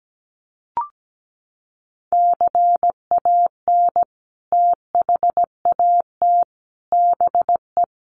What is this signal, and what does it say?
Morse code, heard as a keyed tone, about 17 wpm, 700 Hz. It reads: CAN THAT BE
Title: Sentences from Top 1200 Words 17wpm